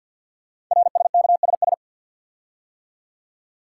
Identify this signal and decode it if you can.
Morse code, heard as a keyed tone, about 50 wpm, 690 Hz. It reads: RSKSS
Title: Code Groups - 5 Letters - 50wpm